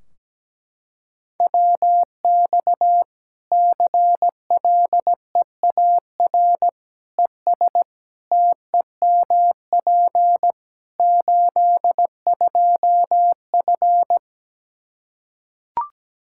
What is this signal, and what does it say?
Morse code, heard as a keyed tone, about 17 wpm, 700 Hz. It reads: WX CLEAR ES TEMP 82F
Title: Practice QSO - Exchange 2 - Info - 17wpm